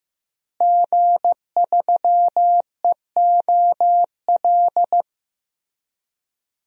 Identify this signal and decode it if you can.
Morse code, heard as a keyed tone, about 15 wpm, 700 Hz. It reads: G3EOL
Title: DXCC-100 Call Signs - 5 Characters - Repeated 3x - 15wpm